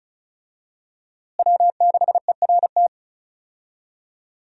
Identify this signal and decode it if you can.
Morse code, heard as a keyed tone, about 36 wpm, 700 Hz. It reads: W6ERT
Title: USA Call Signs - 5 Characters - Repeated 2x - 35wpm